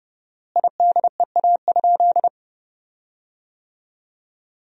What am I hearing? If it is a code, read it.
Morse code, heard as a keyed tone, about 30 wpm, 700 Hz. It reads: IDEA?
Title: 4x Word-Spacing - Sentences from Top 400 Words - 30wpm